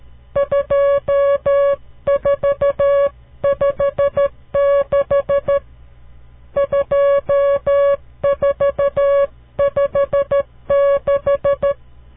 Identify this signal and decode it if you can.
Morse code, heard as a keyed tone, about 13 wpm, 560 Hz. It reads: 2456 2456
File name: CW Learning Programs / farnsworth / unit 5